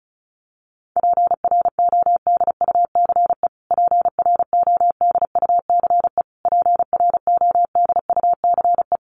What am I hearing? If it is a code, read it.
Morse code, heard as a keyed tone, about 34 wpm, 700 Hz. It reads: PRODUCE PRODUCE PRODUCE
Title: Top 500 Words Repeated 35wpm